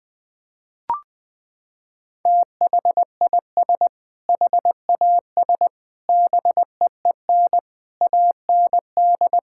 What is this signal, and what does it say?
Morse code, heard as a keyed tone, about 20 wpm, 700 Hz. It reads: THIS HAS BEEN AND
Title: Sentences from Top 1600 Words 20wpm